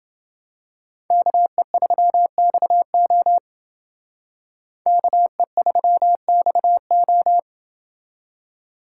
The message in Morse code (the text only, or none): KE3XO KE3XO